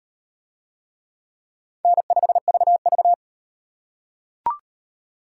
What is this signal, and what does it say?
Morse code, heard as a keyed tone, about 38 wpm, 700 Hz. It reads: N5VV